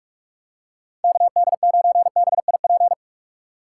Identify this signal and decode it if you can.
Morse code, heard as a keyed tone, about 47 wpm, 700 Hz. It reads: KD9BIP
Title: USA Call Signs - 6 Characters - 3x Character Spacing - 45wpm